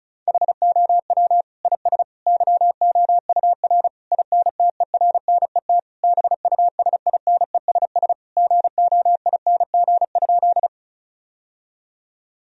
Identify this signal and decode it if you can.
Morse code, heard as a keyed tone, about 35 wpm, 700 Hz. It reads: HOW IS YOUR INTERNET BUSINESS GOING?
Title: Sentences from Top 1000 Words - Review Part 4 - 35wpm